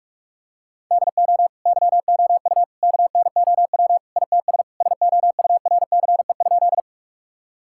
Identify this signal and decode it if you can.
Morse code, heard as a keyed tone, about 46 wpm, 700 Hz. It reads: DO YOU KNOW ITS SOURCE?
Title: Sentences from Top 1000 Words - Review Part 5 - 45wpm